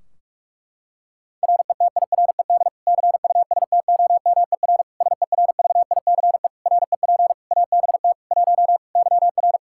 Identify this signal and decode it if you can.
Morse code, heard as a keyed tone, about 46 wpm, 700 Hz. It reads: RETIRED CUSTOMER SERVICE REP ABT 1 YR
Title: Practice QSO - Exchange 3 - More Info - 45wpm